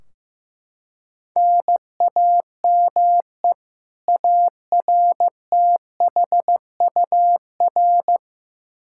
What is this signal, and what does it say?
Morse code, heard as a keyed tone, about 15 wpm, 700 Hz. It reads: NAME ARTHUR
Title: Practice QSO - Exchange 1 - Beginning - 15wpm